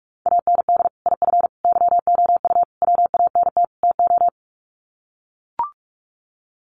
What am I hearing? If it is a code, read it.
Morse code, heard as a keyed tone, about 44 wpm, 700 Hz. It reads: AND IF YOU WANT TO
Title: Sets of 5 Words Encore 45wpm